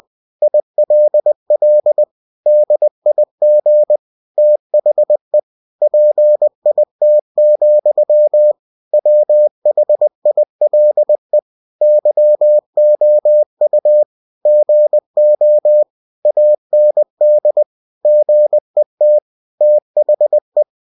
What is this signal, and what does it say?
Morse code, heard as a keyed tone, about 20 wpm, 600 Hz. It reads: ILL DIG THE PIT, WHILE YOU GO AND GET THE
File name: cw-books / winnie-the-pooh / winnie-20wpm-page-0037